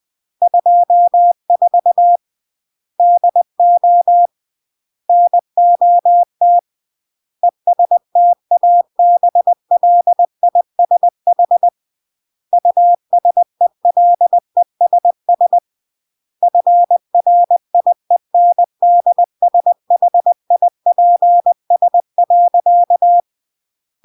Morse code, heard as a keyed tone, about 20 wpm, 700 Hz. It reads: 24 DO NOT ESTABLISH USELESS FRIENDSHIPS.
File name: CW / jodorowsky / 82 commandments 20-20